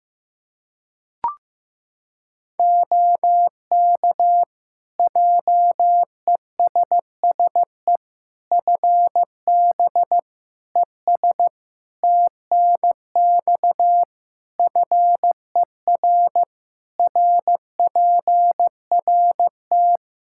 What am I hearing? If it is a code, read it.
Morse code, heard as a keyed tone, about 15 wpm, 700 Hz. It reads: OK JESSE FB ES TNX FER RPRT
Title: Practice QSO - Exchange 2 - Info - 15wpm